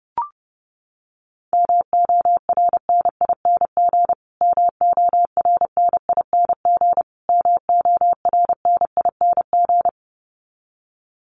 Morse code, heard as a keyed tone, about 30 wpm, 700 Hz. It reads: MORNING MORNING MORNING
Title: Top 500 Words Repeated 30wpm